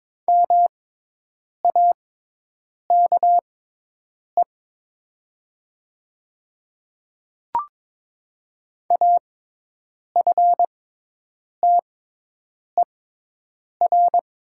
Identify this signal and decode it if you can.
Morse code, heard as a keyed tone, about 22 wpm, spaced as for 8 wpm, 700 Hz. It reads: MAKE AFTER
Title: 6x Character Spacing - Top 300 Words - 22wpm